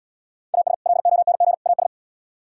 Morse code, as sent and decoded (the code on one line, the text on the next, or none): .- ... / .-.. --- -. --. / .- ...
AS LONG AS